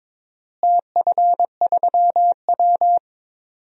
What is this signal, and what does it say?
Morse code, heard as a keyed tone, about 22 wpm, 700 Hz. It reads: TF3W
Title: DXCC-100 Call Signs - 4 Characters - Repeated 3x - 22wpm